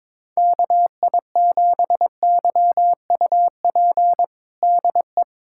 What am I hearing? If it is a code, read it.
Morse code, heard as a keyed tone, about 22 wpm, 700 Hz. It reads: KI7YUP DE
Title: USA Callsign DE Callsign 22wpm